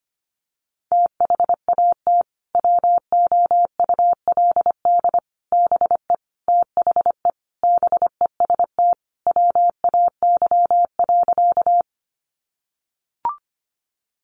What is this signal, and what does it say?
Morse code, heard as a keyed tone, about 25 wpm, 700 Hz. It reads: THAT WOULD BE THE BEST WAY.
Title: Sentences from Top 400 Words 25wpm